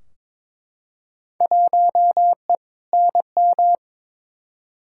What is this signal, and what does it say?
Morse code, heard as a keyed tone, about 22 wpm, 700 Hz. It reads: 1E NM
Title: Field Day Exchanges - 22wpm